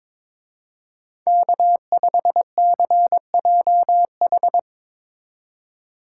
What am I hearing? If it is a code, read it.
Morse code, heard as a keyed tone, about 24 wpm, 700 Hz. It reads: K5CJH
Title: Speed Racing - USA 1x3 Call Signs 22wpm